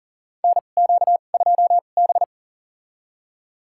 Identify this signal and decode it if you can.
Morse code, heard as a keyed tone, about 37 wpm, spaced as for 28 wpm, 700 Hz. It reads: NQ2B
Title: USA Call Signs - 4 Characters - 2x Character Spacing - 40wpm